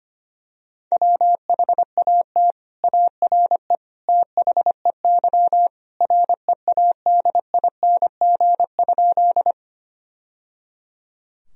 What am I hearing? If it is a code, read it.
Morse code, heard as a keyed tone, about 25 wpm, 700 Hz. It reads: WHAT ARE THEY READING?
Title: Sentences from Top 1000 Words - Review Part 3 - 25wpm